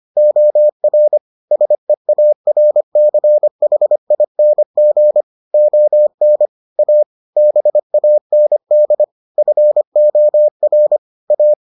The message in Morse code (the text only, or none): OR SEARCHING ON A BAND FOR A